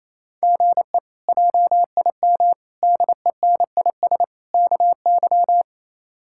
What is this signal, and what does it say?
Morse code, heard as a keyed tone, about 28 wpm, 700 Hz. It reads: GE JIM DENIS KY
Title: K1USN SST QSO Practice - 28wpm